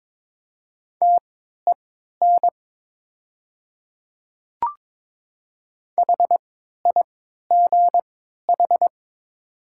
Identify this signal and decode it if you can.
Morse code, heard as a keyed tone, about 23 wpm, spaced as for 13 wpm, 690 Hz. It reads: TEN HIGH
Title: 3x Character Spacing - Top 500 Words - 22wpm